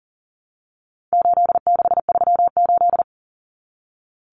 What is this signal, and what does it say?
Morse code, heard as a keyed tone, about 38 wpm, 700 Hz. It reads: 8638
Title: Common Cut Numbers - 4-Digit 40wpm